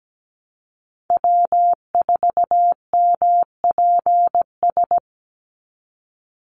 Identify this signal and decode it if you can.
Morse code, heard as a keyed tone, about 17 wpm, 700 Hz. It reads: W4MPS
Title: NA SOTA Chaser Callsigns 17wpm